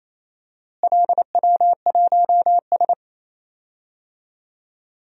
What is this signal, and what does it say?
Morse code, heard as a keyed tone, about 28 wpm, 700 Hz. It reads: LW1S
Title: DXCC-100 Call Signs - 4 Characters - Repeated 2x - 28wpm